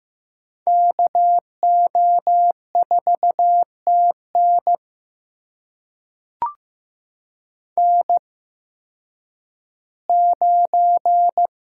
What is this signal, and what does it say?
Morse code, heard as a keyed tone, about 16 wpm, spaced as for 10 wpm, 700 Hz. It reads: KO4TN N9
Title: USA Call Signs - 5 Characters - 8x Character Spacing - 15wpm